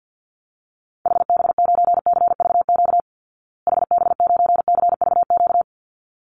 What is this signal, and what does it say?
Morse code, heard as a keyed tone, about 50 wpm, 700 Hz. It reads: HB9CVQ HB9CVQ
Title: Call Signs Repeated 3x - 50wpm